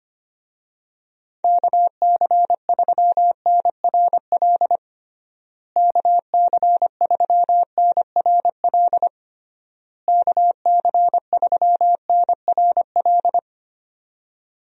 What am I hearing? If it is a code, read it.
Morse code, heard as a keyed tone, about 25 wpm, 700 Hz. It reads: KC3NRL KC3NRL KC3NRL
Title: USA Call Signs - 6 Characters - Repeated 3x - 25wpm